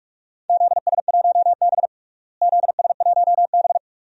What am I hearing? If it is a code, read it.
Morse code, heard as a keyed tone, about 46 wpm, 700 Hz. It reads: ZS1B ZS1B
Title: DXCC-100 Call Signs - 4 Characters - Repeated 2x - 45wpm